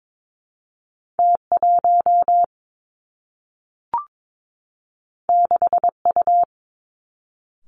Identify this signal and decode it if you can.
Morse code, heard as a keyed tone, about 22 wpm, 700 Hz. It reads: T 1 6 U
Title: Two Challenging Letter-Number Combinations 22wpm